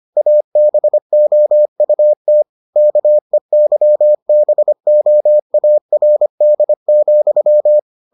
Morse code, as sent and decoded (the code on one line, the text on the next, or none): .- -... --- ..- - / -.- . -.-- -... --- .- .-. -.. --..--
ABOUT KEYBOARD,